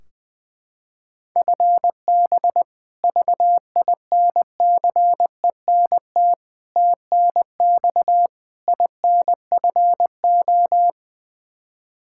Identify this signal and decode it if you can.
Morse code, heard as a keyed tone, about 20 wpm, 700 Hz. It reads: FB VINCENT TNX INFO
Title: Practice QSO - Exchange 3 - More Info - 20wpm